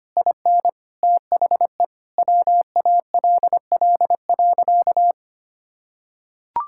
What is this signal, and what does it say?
Morse code, heard as a keyed tone, about 25 wpm, 700 Hz. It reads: IN THE WALL.
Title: Sentences from Top 1600 Words 25wpm